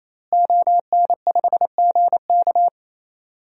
Morse code, heard as a keyed tone, about 28 wpm, 700 Hz. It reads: ON5GK
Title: DXCC-100 Call Signs - 5 Characters - Repeated 3x - 28wpm